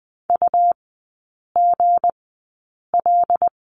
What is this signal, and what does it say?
Morse code, heard as a keyed tone, about 20 wpm, spaced as for 8 wpm, 690 Hz. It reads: UGL
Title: 159 Lesson 26e - Letter g - Three Characters 20wpm